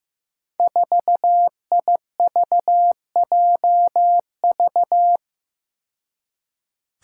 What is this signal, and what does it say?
Morse code, heard as a keyed tone, about 15 wpm, 700 Hz. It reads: 4IVJV
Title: Five Challenging Letter-Number Combinations 15wpm